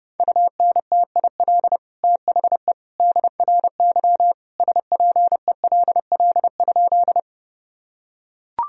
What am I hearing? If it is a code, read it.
Morse code, heard as a keyed tone, about 31 wpm, 700 Hz. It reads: UNTIL THE DRY SPELL?
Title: Sentences from Top 1000 Words - Review Part 3 - 30wpm